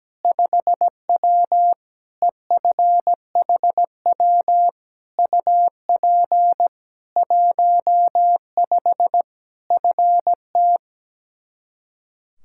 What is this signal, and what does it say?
Morse code, heard as a keyed tone, about 17 wpm, 700 Hz. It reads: 5W EFHW UP 15 FT
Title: K1USN SST QSO Practice 17wpm